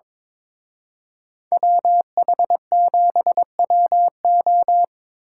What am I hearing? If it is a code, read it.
Morse code, heard as a keyed tone, about 25 wpm, 700 Hz. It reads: WH7WO
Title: Speed Racing - USA 2x2 Call Signs 22wpm